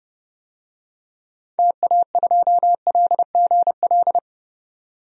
Morse code, heard as a keyed tone, about 32 wpm, 700 Hz. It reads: TA2LGL
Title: DXCC-100 Call Signs - 6 Characters - 30wpm